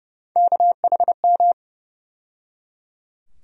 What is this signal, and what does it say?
Morse code, heard as a keyed tone, about 30 wpm, 700 Hz. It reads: KHM
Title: USA Call Sign Postfixes - 3 Letters 30wpm